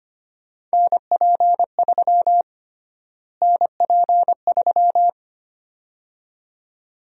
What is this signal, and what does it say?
Morse code, heard as a keyed tone, about 25 wpm, 700 Hz. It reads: NP3 NP3
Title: USA Call Sign Prefixes - Repeated 2x - 25wpm